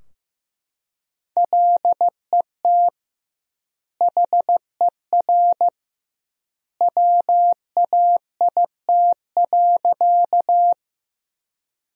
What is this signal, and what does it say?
Morse code, heard as a keyed tone, about 16 wpm, 700 Hz. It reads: LET HER WAIT.
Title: 2x Word-Spacing - Sentences from Top 500 Words - 15wpm